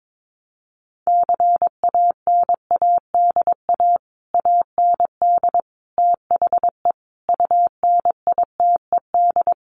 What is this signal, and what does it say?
Morse code, heard as a keyed tone, about 22 wpm, 700 Hz. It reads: CANADA AND THE UNITED
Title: Sets of 5 Words Encore 22wpm